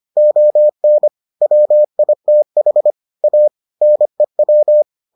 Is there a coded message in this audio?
Yes: Morse code from 0.2 to 4.8 s, heard as a keyed tone, about 25 wpm, 600 Hz. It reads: ON WITH A NEW